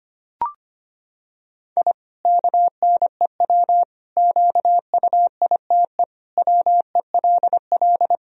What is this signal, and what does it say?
Morse code, heard as a keyed tone, about 25 wpm, 700 Hz. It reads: I KNEW QUITE WELL
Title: Sentences from Top 1000 Words - Review Part 3 - 25wpm